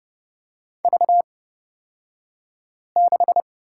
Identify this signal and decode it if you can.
Morse code, heard as a keyed tone, about 30 wpm, spaced as for 5 wpm, 700 Hz. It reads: V6
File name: Lesson 38d - Letter q - Two Characters 30wpm